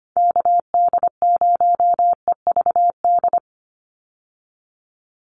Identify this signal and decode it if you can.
Morse code, heard as a keyed tone, about 25 wpm, 700 Hz. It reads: KD0EVD
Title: USA Call Signs 25wpm